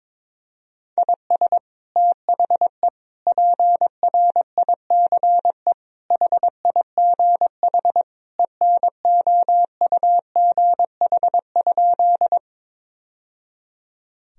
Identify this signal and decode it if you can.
Morse code, heard as a keyed tone, about 22 wpm, 700 Hz. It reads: IS THE PRICE HIGH ENOUGH?
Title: Sentences from Top 1000 Words - Review Part 4 - 22wpm